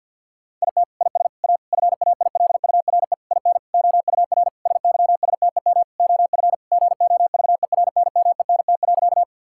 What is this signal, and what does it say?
Morse code, heard as a keyed tone, about 50 wpm, 700 Hz. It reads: IT IS A FAILURE IN OUR SYSTEM OF GOVERNMENT.
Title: Sentences from Top 1000 Words - Review Part 6 - 50wpm